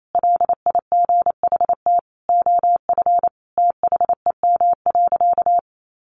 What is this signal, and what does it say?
Morse code, heard as a keyed tone, about 28 wpm, 700 Hz. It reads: LIGHT OF THEM.
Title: Sentences from Top 1300 Words 28wpm